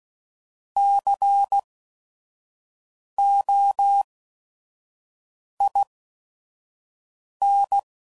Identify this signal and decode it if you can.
Morse code, heard as a keyed tone, about 16 wpm, spaced as for 5 wpm, 800 Hz. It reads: COIN
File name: Prefixes / By First Letter / Cs